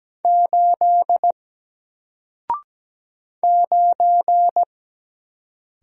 Morse code, heard as a keyed tone, about 17 wpm, 700 Hz. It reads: 8 9
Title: Single Letter-Number Alt 17wpm